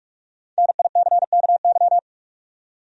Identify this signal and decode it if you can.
Morse code, heard as a keyed tone, about 46 wpm, 700 Hz. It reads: NICKY
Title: USA Common Names - 45wpm